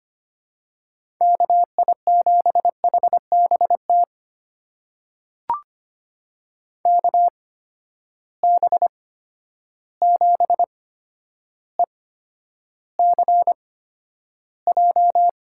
Morse code, heard as a keyed tone, about 25 wpm, spaced as for 11 wpm, 700 Hz. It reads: KI7HBT KB7ECJ
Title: USA Call Signs - 6 Characters - 8x Character Spacing - 25wpm